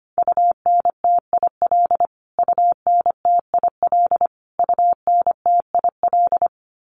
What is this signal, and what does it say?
Morse code, heard as a keyed tone, about 25 wpm, 700 Hz. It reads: UNTIL UNTIL UNTIL
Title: Top 500 Words Repeated 25wpm